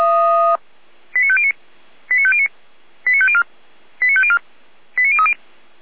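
Five-tone selective call (ZVEI), heard as quick runs of five tones, tones of about 70 ms. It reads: 89509 89509 89504 89504 89030
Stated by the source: ZVEI five-tone call received by SDR, from the audio server of feuerwehr-orlamuende.org (fire brigade Orlamünde).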